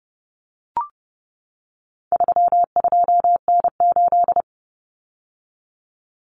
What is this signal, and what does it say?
Morse code, heard as a keyed tone, about 30 wpm, 700 Hz. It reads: 32N8
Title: Common Cut Numbers - 4-Digit 30wpm